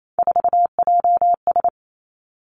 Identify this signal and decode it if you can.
Morse code, heard as a keyed tone, about 28 wpm, 700 Hz. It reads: 4JS
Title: Three Challenging Letter-Number Combinations 28wpm